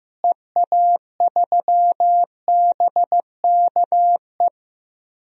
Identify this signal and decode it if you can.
Morse code, heard as a keyed tone, about 15 wpm, 700 Hz. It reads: EA3BKE
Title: DXCC-100 Call Signs - 6 Characters - Repeated 3x - 15wpm